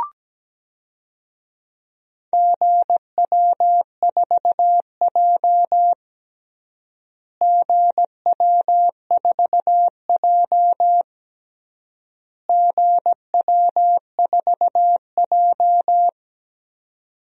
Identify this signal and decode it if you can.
Morse code, heard as a keyed tone, about 17 wpm, 700 Hz. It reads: GW4J GW4J GW4J
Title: DXCC-100 Call Signs - 4 Characters - Repeated 3x - 17wpm